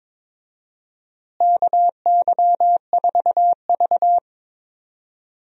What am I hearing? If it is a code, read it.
Morse code, heard as a keyed tone, about 22 wpm, 700 Hz. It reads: KY4V
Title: USA Call Signs - 4 Characters - Repeated 2x - 22wpm